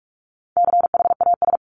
Morse code, heard as a keyed tone, about 44 wpm, 700 Hz. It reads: CHAS
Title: CWT Exchange 45wpm